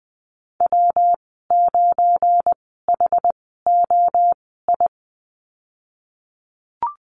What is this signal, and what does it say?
Morse code, heard as a keyed tone, about 20 wpm, spaced as for 15 wpm, 700 Hz. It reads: W9HOI
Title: 186 Lesson 30g - Letter b - Call Signs 20wpm